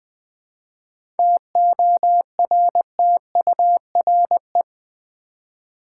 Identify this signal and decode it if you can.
Morse code, heard as a keyed tone, about 20 wpm, 700 Hz. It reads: TORTURE
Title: 7 Letter Words - First 3 Letters Given - 20wpm